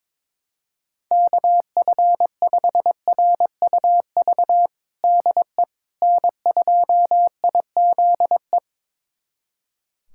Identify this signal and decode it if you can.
Morse code, heard as a keyed tone, about 21 wpm, 700 Hz. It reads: KF5RUV DE N2IZE
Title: USA Callsign DE Callsign 22wpm